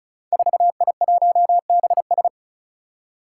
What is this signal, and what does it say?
Morse code, heard as a keyed tone, about 36 wpm, 700 Hz. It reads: 4I1BS
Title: Five Challenging Letter-Number Combinations 35wpm